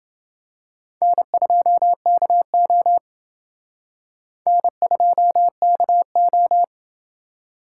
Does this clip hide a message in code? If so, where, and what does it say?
Yes: Morse code from 1.0 to 6.6 s, heard as a keyed tone, about 27 wpm, 700 Hz. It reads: N2KO N2KO